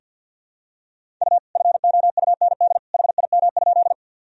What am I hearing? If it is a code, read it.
Morse code, heard as a keyed tone, about 50 wpm, 700 Hz. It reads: U FOUND HIM?
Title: Sentences from Top 300 Words and Abbreviations 50wpm